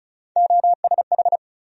Morse code, heard as a keyed tone, about 36 wpm, 700 Hz. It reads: OSH
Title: USA Call Sign Postfixes - 3 Letters 35wpm